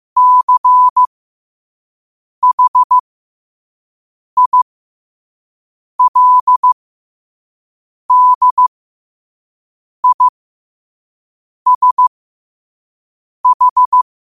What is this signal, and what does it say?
Morse code, heard as a keyed tone, about 15 wpm, spaced as for 5 wpm, 1000 Hz. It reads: CHILDISH
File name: qotdfiles / 5wpm